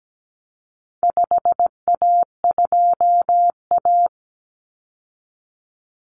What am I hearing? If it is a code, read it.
Morse code, heard as a keyed tone, about 17 wpm, 700 Hz. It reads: 5A2A
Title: Common Cut Numbers - 4-Digit 17wpm